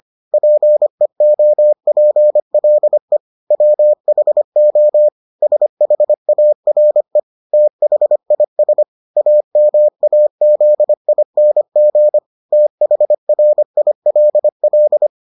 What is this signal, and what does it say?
Morse code, heard as a keyed tone, about 25 wpm, 600 Hz. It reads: PEOPLE WHO SHARE THIS AMAZING THRILL